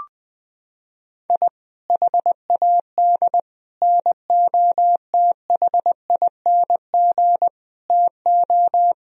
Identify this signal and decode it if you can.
Morse code, heard as a keyed tone, about 20 wpm, 700 Hz. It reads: I HAD NOTHING TO
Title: Sentences from Top 1000 Words - Review Part 1 - 20wpm